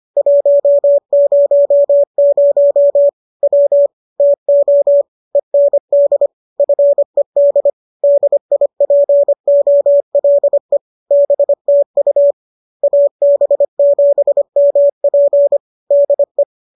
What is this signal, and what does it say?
Morse code, heard as a keyed tone, about 25 wpm, 570 Hz. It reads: 100 W TO END FED DIPOLE BTU AB7MP DE
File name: qso206 25